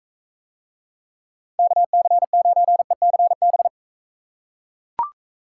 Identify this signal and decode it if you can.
Morse code, heard as a keyed tone, about 38 wpm, 700 Hz. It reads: KC9ECB